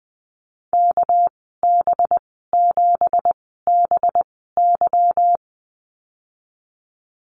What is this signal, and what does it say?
Morse code, heard as a keyed tone, about 19 wpm, spaced as for 14 wpm, 700 Hz. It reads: KB7BY
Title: 200 Lesson 32g - Letter k - Call Signs 20wpm